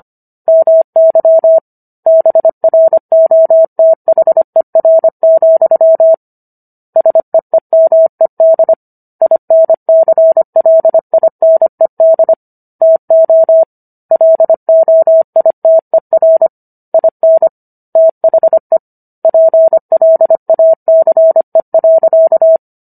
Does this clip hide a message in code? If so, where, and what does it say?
Yes: Morse code from 0.5 to 22.6 s, heard as a keyed tone, about 25 wpm, 650 Hz. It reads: MY BROTHER, SEEMED INCLINED TO LOITER IN THE PLACE.